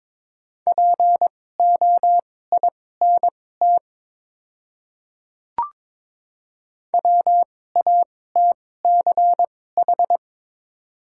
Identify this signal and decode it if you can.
Morse code, heard as a keyed tone, about 22 wpm, spaced as for 16 wpm, 700 Hz. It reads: POINT WATCH
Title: 2x Character Spacing - Top 500 Words - 22wpm